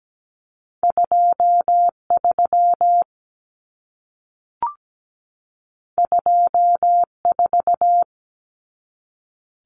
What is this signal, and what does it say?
Morse code, heard as a keyed tone, about 17 wpm, 700 Hz. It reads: 23 24
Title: Common Cut Numbers - 2-Digit 17wpm